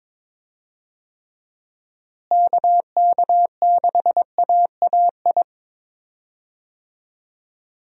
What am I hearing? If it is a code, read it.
Morse code, heard as a keyed tone, about 22 wpm, 700 Hz. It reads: KK6AAI
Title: USA Call Signs - 6 Characters - Repeated 3x - 22wpm